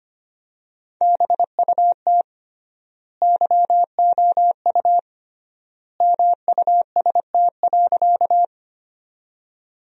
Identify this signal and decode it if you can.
Morse code, heard as a keyed tone, about 25 wpm, 700 Hz. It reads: BUT YOU MUST.
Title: 3x Word-Spacing - Sentences from Top 200 Words - 25wpm